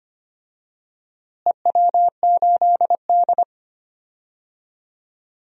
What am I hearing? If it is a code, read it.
Morse code, heard as a keyed tone, about 25 wpm, 700 Hz. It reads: EW8D
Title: DXCC-100 Call Signs - 4 Characters - Repeated 2x - 25wpm